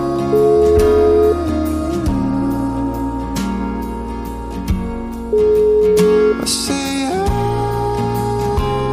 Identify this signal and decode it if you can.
Ringback tone (Europe, 425 Hz), with noise.